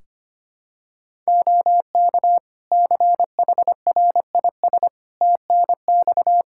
Morse code, heard as a keyed tone, about 25 wpm, 700 Hz. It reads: OK CHRIS TNX
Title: Practice QSO - Exchange 4 - Ending - 25wpm